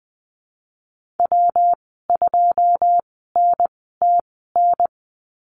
Morse code, heard as a keyed tone, about 20 wpm, spaced as for 15 wpm, 700 Hz. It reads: W2NTN
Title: 179 Lesson 29g - Slash - Call Signs 20wpm